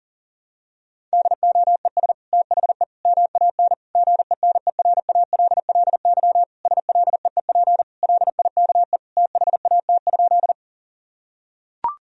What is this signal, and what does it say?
Morse code, heard as a keyed tone, about 40 wpm, 700 Hz. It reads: DOES THE MAN GENERALLY SLEEP LIKE THAT?
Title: Sentences from Top 1000 Words - Review Part 4 - 40wpm